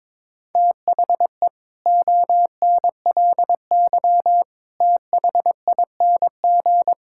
Morse code, heard as a keyed tone, about 22 wpm, 700 Hz. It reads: THE ONLY THING
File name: Sets of 9 Words - 22wpm